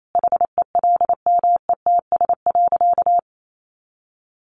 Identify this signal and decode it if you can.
Morse code, heard as a keyed tone, about 28 wpm, 700 Hz. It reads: HELMETS.
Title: A Sound Of Thunder 28wpm